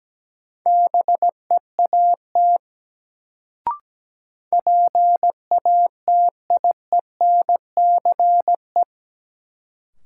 Morse code, heard as a keyed tone, about 17 wpm, 700 Hz. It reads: BEAT PATIENCE
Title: Sentences from Top 1300 Words - New Word Drill - 17wpm